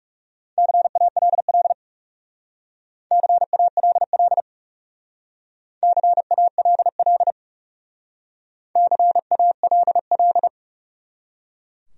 Morse code, heard as a keyed tone, about 38 wpm, 700 Hz. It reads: CALL CALL CALL CALL